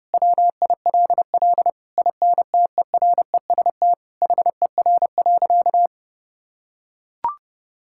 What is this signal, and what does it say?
Morse code, heard as a keyed tone, about 32 wpm, 700 Hz. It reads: WILL INTEREST HER.
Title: Sentences from Top 1000 Words - Review Part 5 - 30wpm